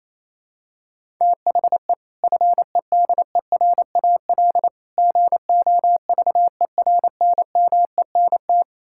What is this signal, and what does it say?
Morse code, heard as a keyed tone, about 28 wpm, 700 Hz. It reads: THE FEDERAL GOVERNMENT
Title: Sentences from Top 1000 Words - Review Part 5 - 28wpm